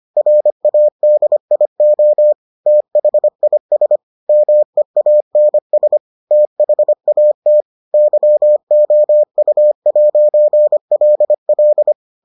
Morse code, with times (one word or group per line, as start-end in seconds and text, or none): 0.2-2.3 RADIO
2.7-4.0 THIS
4.3-6.0 MEANS
6.3-7.6 THAT
7.9-11.9 YOU'LL